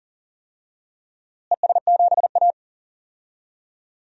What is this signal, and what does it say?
Morse code, heard as a keyed tone, about 40 wpm, 700 Hz. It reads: ES7A
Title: DXCC-100 Call Signs - 4 Characters - Repeated 3x - 40wpm